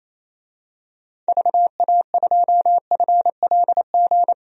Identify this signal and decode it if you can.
Morse code, heard as a keyed tone, about 28 wpm, 700 Hz. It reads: VA2FLG
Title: DXCC-100 Call Signs - 6 Characters - 28wpm